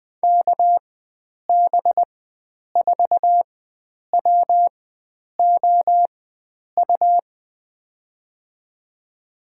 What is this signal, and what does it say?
Morse code, heard as a keyed tone, about 20 wpm, spaced as for 9 wpm, 700 Hz. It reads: KB4WOU
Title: USA Call Signs - 4x Character Spacing - 20wpm